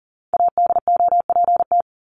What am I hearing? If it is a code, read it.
Morse code, heard as a keyed tone, about 40 wpm, 700 Hz. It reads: ADOPT
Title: ED Suffix - Top 100 Words with Base Words 40wpm